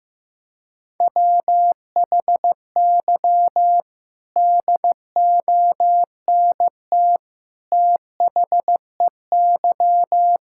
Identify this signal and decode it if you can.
Morse code, heard as a keyed tone, about 15 wpm, 700 Hz. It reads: WHY DONT THEY
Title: Sentences from Top 1000 Words - Review Part 5 - 15wpm